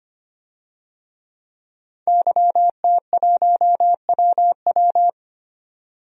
Morse code, heard as a keyed tone, about 25 wpm, 700 Hz. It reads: YT1WW
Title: DXCC-100 Call Signs - 5 Characters - 25wpm